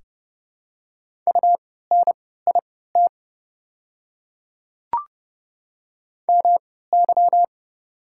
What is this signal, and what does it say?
Morse code, heard as a keyed tone, about 29 wpm, spaced as for 17 wpm, 700 Hz. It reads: UNIT MY
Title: 3x Character Spacing - Top 500 Words - 30wpm